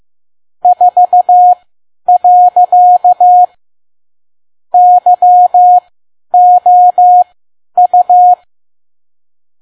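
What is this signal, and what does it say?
Morse code, heard as a keyed tone, about 15 wpm, spaced as for 10 wpm, 720 Hz. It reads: 4. YOU
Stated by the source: Morse code practice at 10 WPM.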